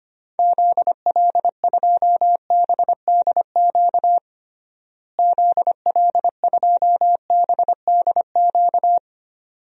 Morse code, heard as a keyed tone, about 25 wpm, 700 Hz. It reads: ZL2BDQ ZL2BDQ
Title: DXCC-100 Call Signs - 6 Characters - Repeated 2x - 25wpm